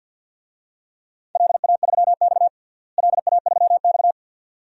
Morse code, heard as a keyed tone, about 51 wpm, 700 Hz. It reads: LA3X LA3X
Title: DXCC-100 Call Signs - 4 Characters - Repeated 2x - 50wpm